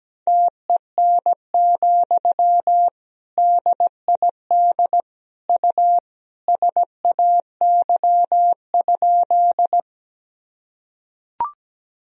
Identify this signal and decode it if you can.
Morse code, heard as a keyed tone, about 17 wpm, 700 Hz. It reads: TEN, DID U SAY?
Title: Sentences from Top 400 Words and Abbreviations 17wpm